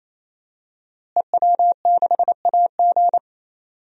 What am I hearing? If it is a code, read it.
Morse code, heard as a keyed tone, about 28 wpm, 690 Hz. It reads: EW6AG
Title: DXCC-100 Call Signs - 5 Characters - Repeated 3x - 28wpm